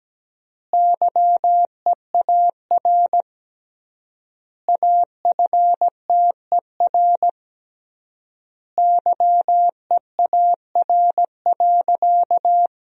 Morse code, heard as a keyed tone, about 17 wpm, 700 Hz. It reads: YEAR AFTER YEAR.
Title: 3x Word-Spacing - Sentences from Top 300 Words - 17wpm